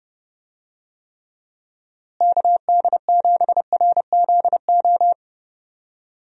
Morse code, heard as a keyed tone, about 30 wpm, 700 Hz. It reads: KD7RZO
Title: USA Call Signs - 6 Characters - Repeated 3x - 30wpm